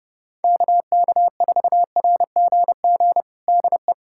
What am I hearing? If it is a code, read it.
Morse code, heard as a keyed tone, about 30 wpm, 700 Hz. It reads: KK4RGG DE
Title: USA Callsign DE Callsign 30wpm